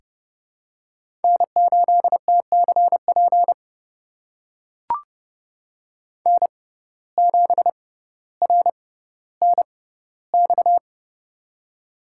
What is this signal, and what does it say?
Morse code, heard as a keyed tone, about 30 wpm, spaced as for 15 wpm, 700 Hz. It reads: N8TCP N7RNX
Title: USA Call Signs - 5 Characters - 6x Character Spacing - 30wpm